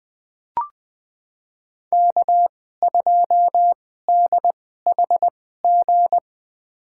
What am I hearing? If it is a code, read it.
Morse code, heard as a keyed tone, about 20 wpm, spaced as for 14 wpm, 700 Hz. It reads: K2DHG
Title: USA Call Signs - 5 Characters - 2x Character Spacing - 20wpm